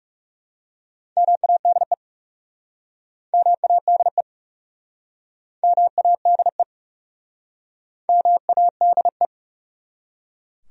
Morse code, heard as a keyed tone, about 38 wpm, 700 Hz. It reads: MADE MADE MADE MADE
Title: Speed-Racing Top 200 Words 30wpm